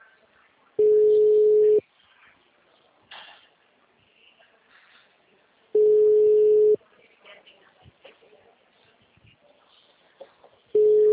Ringback tone (Europe, 425 Hz).
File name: out- -2013